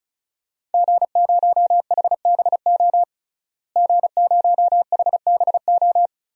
Morse code, heard as a keyed tone, about 36 wpm, 700 Hz. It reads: G0HBO G0HBO
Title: DXCC-100 Call Signs - 5 Characters - Repeated 3x - 35wpm